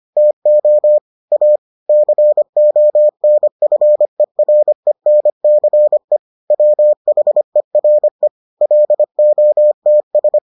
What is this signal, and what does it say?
Morse code, heard as a keyed tone, about 25 wpm, 600 Hz. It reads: TO A CONFERENCE WHERE LOTS